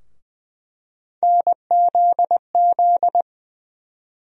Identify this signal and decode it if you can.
Morse code, heard as a keyed tone, about 20 wpm, 700 Hz. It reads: NZZ
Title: USA Call Sign Postfixes - 3 Letters 20wpm